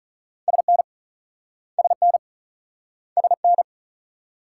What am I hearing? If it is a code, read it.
Morse code, heard as a keyed tone, about 45 wpm, 700 Hz. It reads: SN SN SN